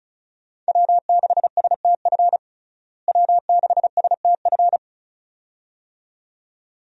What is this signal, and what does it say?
Morse code, heard as a keyed tone, about 36 wpm, 700 Hz. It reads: W6STF W6STF
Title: USA Call Signs - 5 Characters - Repeated 2x - 35wpm